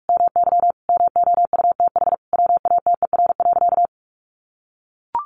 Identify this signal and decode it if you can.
Morse code, heard as a keyed tone, about 44 wpm, 700 Hz. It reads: MY MOUTH WATER.
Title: Sentences from Top 1500 Words 45wpm